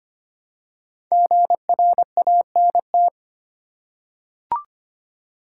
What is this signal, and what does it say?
Morse code, heard as a keyed tone, about 25 wpm, 700 Hz. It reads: GRANT
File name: 5 Letter Words - First 3 Letters Given - 25wpm